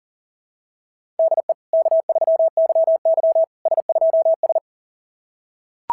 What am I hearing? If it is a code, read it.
Morse code, heard as a keyed tone, about 40 wpm, 660 Hz. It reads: DE K3YY S2S
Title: SOTA QSO Practice 40wpm